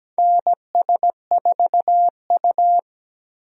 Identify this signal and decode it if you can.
Morse code, heard as a keyed tone, about 17 wpm, 700 Hz. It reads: NS4U
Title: USA Call Signs - 4 Characters - Repeated 3x - 17wpm